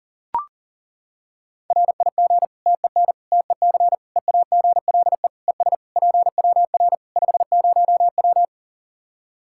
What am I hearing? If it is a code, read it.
Morse code, heard as a keyed tone, about 40 wpm, 700 Hz. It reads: RIG TEN TEC EAGLE ES PWR 50W
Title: Practice QSO - Exchange 2 - Info - 40wpm